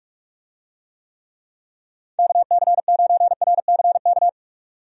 Morse code, heard as a keyed tone, about 44 wpm, 700 Hz. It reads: KC9RCK
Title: USA Call Signs - 6 Characters - 45wpm